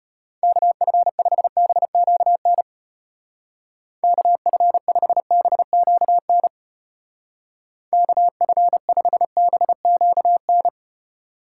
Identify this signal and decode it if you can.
Morse code, heard as a keyed tone, about 32 wpm, 700 Hz. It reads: KF5BQN KF5BQN KF5BQN